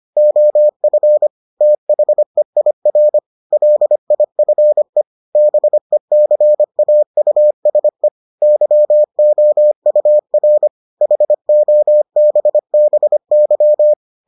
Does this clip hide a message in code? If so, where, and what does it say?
Yes: Morse code from 0.2 to 13.9 s, heard as a keyed tone, about 25 wpm, 600 Hz. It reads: OF THEIR LIFE BECAUSE YOUR HOBBY